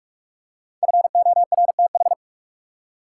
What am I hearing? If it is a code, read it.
Morse code, heard as a keyed tone, about 44 wpm, 700 Hz. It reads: FORTH
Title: Five Letter Words - 45wpm